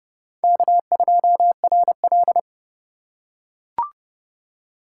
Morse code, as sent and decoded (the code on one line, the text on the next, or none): -.- ..--- .-. .-..
K2RL